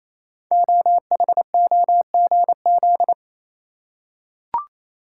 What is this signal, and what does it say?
Morse code, heard as a keyed tone, about 27 wpm, 700 Hz. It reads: OHOGZ
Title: Code Groups - 5 Letters - 28wpm